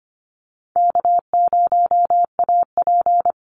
Morse code, heard as a keyed tone, about 25 wpm, 700 Hz. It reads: K0AP
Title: Call Signs Repeated 2x - 25wpm